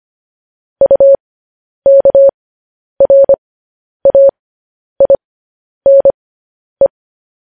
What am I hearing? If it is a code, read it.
Morse code, heard as a keyed tone, about 25 wpm, spaced as for 10 wpm, 550 Hz. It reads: UKRAINE
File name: podcast / News headlines-1x-25x10